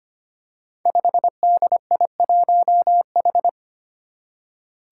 Morse code, heard as a keyed tone, about 25 wpm, 700 Hz. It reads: 5DI1H
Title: Five Challenging Letter-Number Combinations 25wpm